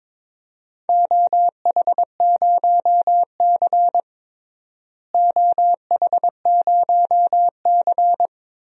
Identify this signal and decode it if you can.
Morse code, heard as a keyed tone, about 22 wpm, 700 Hz. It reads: OH0C OH0C
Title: DXCC-100 Call Signs - 4 Characters - Repeated 2x - 22wpm